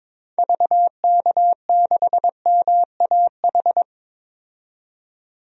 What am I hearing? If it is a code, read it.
Morse code, heard as a keyed tone, about 21 wpm, 700 Hz. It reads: VK6MAH
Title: DXCC-100 Call Signs - 6 Characters - 22wpm